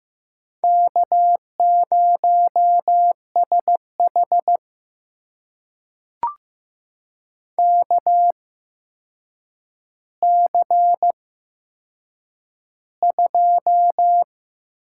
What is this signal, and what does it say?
Morse code, heard as a keyed tone, about 15 wpm, spaced as for 7 wpm, 700 Hz. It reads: K0SH KC2
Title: USA Call Signs - 8x Character Spacing - 15wpm